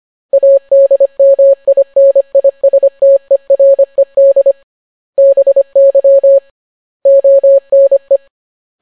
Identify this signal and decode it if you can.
Morse code, heard as a keyed tone, about 25 wpm, 550 Hz. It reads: ADMINISTERED BY ONE